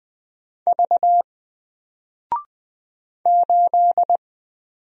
Morse code, heard as a keyed tone, about 20 wpm, 700 Hz. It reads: V 8
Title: Single Character (Letter:Number:Punctuation:Prosign) 20wpm